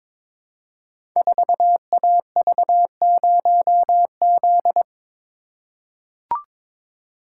Morse code, heard as a keyed tone, about 22 wpm, 700 Hz. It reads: 4AV0Z
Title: Code Groups - 5 Letter:Numbers - 22wpm